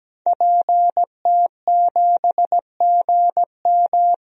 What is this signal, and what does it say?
Morse code, heard as a keyed tone, about 17 wpm, 700 Hz. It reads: PT7GM
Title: DXCC-100 Call Signs - 5 Characters - 17wpm